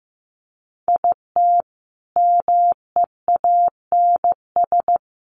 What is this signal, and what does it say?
Morse code, heard as a keyed tone, about 15 wpm, 700 Hz. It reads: IT MEANS
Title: Sets of 5 Words Encore2 15wpm